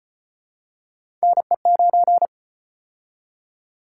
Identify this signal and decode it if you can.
Morse code, heard as a keyed tone, about 37 wpm, 700 Hz. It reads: NE9